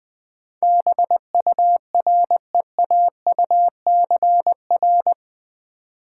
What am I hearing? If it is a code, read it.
Morse code, heard as a keyed tone, about 20 wpm, 700 Hz. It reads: BUREAUCR
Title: Build A Word - 4 Letters At A Time - 11-Letter Words - 20wpm